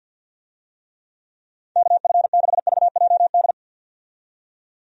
Morse code, heard as a keyed tone, about 51 wpm, 700 Hz. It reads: KF6VJD
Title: USA Call Signs - 6 Characters - 50wpm